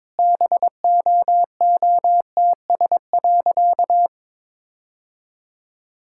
Morse code, heard as a keyed tone, about 21 wpm, 700 Hz. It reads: BOOTS.
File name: practice / 22wpm / Easy Sentences with Hints - 22wpm